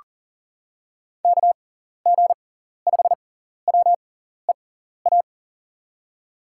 Morse code, heard as a keyed tone, about 39 wpm, spaced as for 14 wpm, 700 Hz. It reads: KG5WEA
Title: USA Call Signs - 6 Characters - 6x Character Spacing - 40wpm